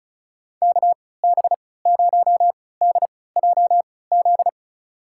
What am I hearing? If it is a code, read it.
Morse code, heard as a keyed tone, about 36 wpm, spaced as for 20 wpm, 700 Hz. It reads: KB0DJZ
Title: USA Call Signs - 6 Characters - 3x Character Spacing - 35wpm